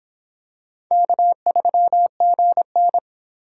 Morse code, heard as a keyed tone, about 24 wpm, 700 Hz. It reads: K3GN